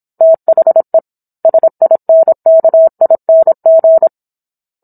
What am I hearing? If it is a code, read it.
Morse code, heard as a keyed tone, about 27 wpm, 650 Hz. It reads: THE SINKING